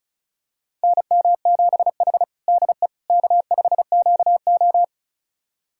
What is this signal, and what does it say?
Morse code, heard as a keyed tone, about 36 wpm, 700 Hz. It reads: NM7H DE K5QO
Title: USA Callsign DE Callsign - 4 Characters - 35wpm